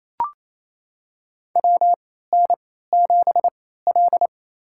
Morse code, heard as a keyed tone, about 27 wpm, spaced as for 16 wpm, 700 Hz. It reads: WN7L
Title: USA Call Signs - 4 Characters - 3x Character Spacing - 28wpm